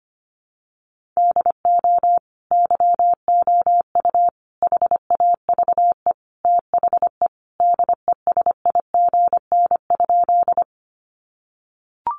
Morse code, heard as a keyed tone, about 25 wpm, 700 Hz. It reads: DO YOU HAVE THE DESIGN?
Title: Sentences from Top 800 Words 25wpm